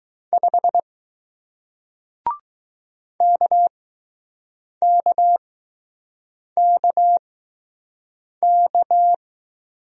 Morse code, heard as a keyed tone, about 21 wpm, spaced as for 6 wpm, 700 Hz. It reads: 5 KKKK